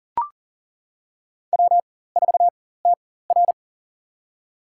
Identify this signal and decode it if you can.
Morse code, heard as a keyed tone, about 40 wpm, spaced as for 19 wpm, 700 Hz. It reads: W4TR
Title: USA Call Signs - 4 Characters - 4x Character Spacing - 40wpm